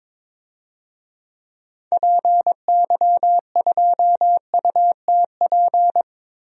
Morse code, heard as a keyed tone, about 22 wpm, 700 Hz. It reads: PY2UTP
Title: DXCC-100 Call Signs - 6 Characters - 22wpm